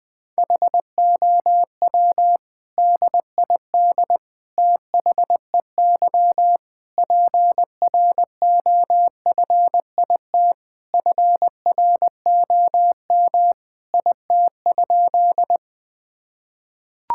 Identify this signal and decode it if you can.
Morse code, heard as a keyed tone, about 20 wpm, 700 Hz. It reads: HOW DID THEY PROFIT FROM IT?
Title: Sentences from Top 1000 Words - Review Part 5 - 20wpm